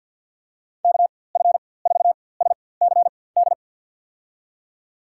Morse code, heard as a keyed tone, about 52 wpm, spaced as for 23 wpm, 700 Hz. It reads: KF4SCD
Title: USA Call Signs - 4x Character Spacing - 50wpm